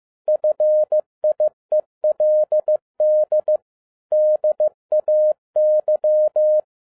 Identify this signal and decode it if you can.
Morse code, heard as a keyed tone, about 15 wpm, 610 Hz. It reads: FIELD DAY
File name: 2011-10-mp3 / n02-FIELD DAY CONTEST QRV FROM SAKAE VILLAGE 15wpm